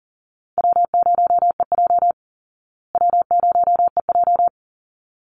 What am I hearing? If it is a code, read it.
Morse code, heard as a keyed tone, about 40 wpm, 700 Hz. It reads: W0EJ W0EJ
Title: Easy Call Signs Repeated 2x - 40wpm